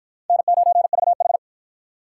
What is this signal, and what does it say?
Morse code, heard as a keyed tone, about 47 wpm, 700 Hz. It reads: N9VH